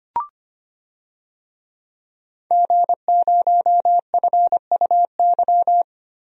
Morse code, heard as a keyed tone, about 25 wpm, 700 Hz. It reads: G0FUY
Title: DXCC-100 Call Signs - 5 Characters - Repeated 2x - 25wpm